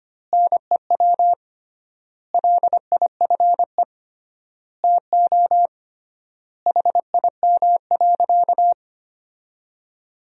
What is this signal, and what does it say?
Morse code, heard as a keyed tone, about 25 wpm, 700 Hz. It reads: NEW LIFE TO HIM.
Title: 3x Word-Spacing - Sentences from Top 300 Words - 25wpm